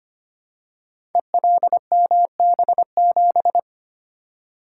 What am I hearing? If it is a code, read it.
Morse code, heard as a keyed tone, about 25 wpm, 700 Hz. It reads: ELMB7
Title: Code Groups - 5 Letter:Numbers - 25wpm